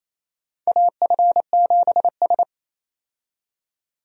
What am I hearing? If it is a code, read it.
Morse code, heard as a keyed tone, about 27 wpm, 700 Hz. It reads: AF7S
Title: USA Call Signs - 4 Characters - 28wpm